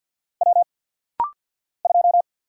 Morse code, heard as a keyed tone, about 50 wpm, 700 Hz. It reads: W 2
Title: Single Letter-Number - Rapid-Fire 50wpm